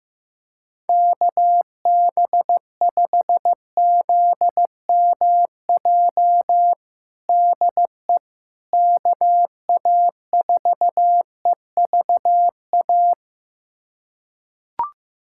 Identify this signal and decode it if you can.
Morse code, heard as a keyed tone, about 16 wpm, 700 Hz. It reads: KB5ZMJ DE KA4EVA
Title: USA Callsign DE Callsign - 6 Characters - 15wpm